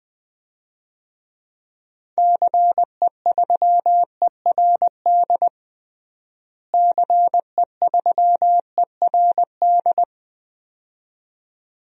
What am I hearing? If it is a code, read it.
Morse code, heard as a keyed tone, about 20 wpm, 700 Hz. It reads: CE3ERD CE3ERD
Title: DXCC-100 Call Signs - 6 Characters - Repeated 2x - 20wpm